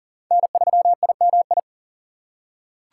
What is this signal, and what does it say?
Morse code, heard as a keyed tone, about 39 wpm, 700 Hz. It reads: N3IMI
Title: USA Call Signs - 5 Characters - 2x Character Spacing - 40wpm